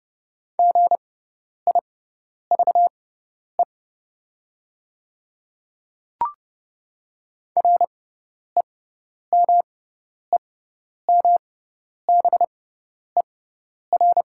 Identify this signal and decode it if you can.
Morse code, heard as a keyed tone, about 30 wpm, spaced as for 10 wpm, 700 Hz. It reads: GIVE REMEMBER
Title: 6x Character Spacing - Top 500 Words - 30wpm